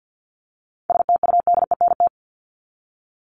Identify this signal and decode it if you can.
Morse code, heard as a keyed tone, about 50 wpm, 700 Hz. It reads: STUDENT
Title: Top 751-1000 Words 50wpm